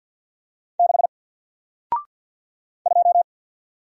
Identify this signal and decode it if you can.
Morse code, heard as a keyed tone, about 50 wpm, 700 Hz. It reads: 6 2
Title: Single Letter-Number Alt 50wpm